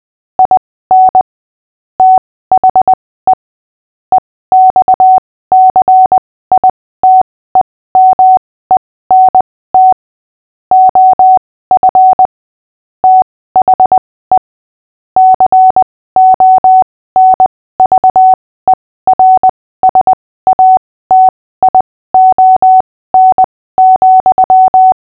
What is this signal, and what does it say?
Morse code, heard as a keyed tone, about 20 wpm, spaced as for 15 wpm, 750 Hz. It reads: IN THE EXCITEMENT OF THE CONVERSATION,
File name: cw / fists / Short-Story-6-15wpm